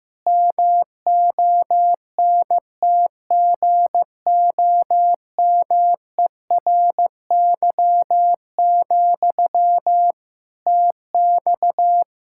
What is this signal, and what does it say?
Morse code, heard as a keyed tone, about 15 wpm, 700 Hz. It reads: MONTGOMERY, TX
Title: Practice QSO - Exchange 1 - Beginning - 15wpm